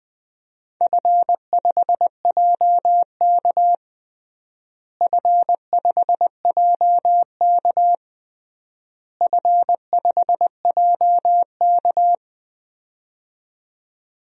Morse code, heard as a keyed tone, about 21 wpm, 700 Hz. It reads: F5JK F5JK F5JK
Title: DXCC-100 Call Signs - 4 Characters - Repeated 3x - 20wpm